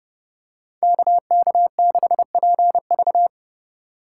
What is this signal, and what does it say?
Morse code, heard as a keyed tone, about 28 wpm, 700 Hz. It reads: KK6PV